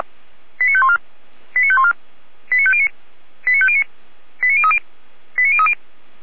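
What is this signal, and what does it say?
Five-tone selective call (ZVEI), heard as quick runs of five tones, tones of about 70 ms. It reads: 89514 89514 89509 89509 89030 89030
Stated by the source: Five-tone ZVEI selective call with code 89001, received by SDR and logged by feuerwehr-orlamuende fire brigade.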